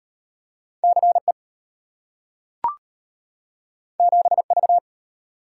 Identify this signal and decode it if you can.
Morse code, heard as a keyed tone, about 33 wpm, 700 Hz. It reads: CE 7V